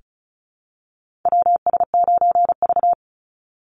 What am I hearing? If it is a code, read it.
Morse code, heard as a keyed tone, about 34 wpm, 700 Hz. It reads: WS9V
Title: Call Signs Repeated 3x - 35wpm